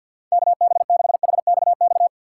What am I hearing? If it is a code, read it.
Morse code, heard as a keyed tone, about 51 wpm, 700 Hz. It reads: K B 6 H X X
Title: USA Call Signs - 6 Characters - 6x Character Spacing - 50wpm